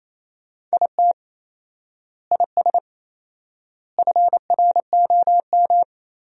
Morse code, heard as a keyed tone, about 28 wpm, 700 Hz. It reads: IT IS FROM
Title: 4x Word-Spacing - Sentences from Top 300 Words - 28wpm